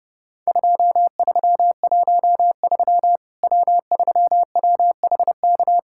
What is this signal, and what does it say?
Morse code, heard as a keyed tone, about 31 wpm, 700 Hz. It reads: 2313 W3WHK
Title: Worked All Europe DX Contest - QTC Exchange - 30wpm